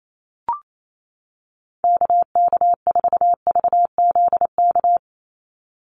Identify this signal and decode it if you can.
Morse code, heard as a keyed tone, about 28 wpm, 700 Hz. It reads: KK4VZK
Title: USA Call Signs 28wpm